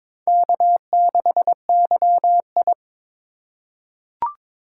Morse code, heard as a keyed tone, about 22 wpm, 700 Hz. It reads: K6YI
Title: USA Call Signs - 4 Characters - 4x Character Spacing - 22wpm